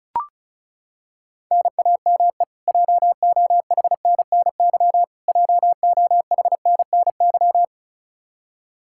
Morse code, heard as a keyed tone, about 36 wpm, 700 Hz. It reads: NAME JOHNNY JOHNNY
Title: Practice QSO - Exchange 1 - Beginning - 35wpm